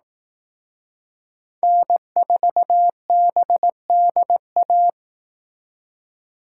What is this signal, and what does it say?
Morse code, heard as a keyed tone, about 18 wpm, 700 Hz. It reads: N4BDA
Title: Speed Racing - USA 1x3 Call Signs 15wpm